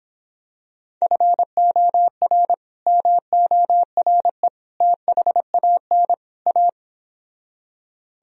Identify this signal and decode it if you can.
Morse code, heard as a keyed tone, about 26 wpm, 700 Hz. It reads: FOR MORE THAN A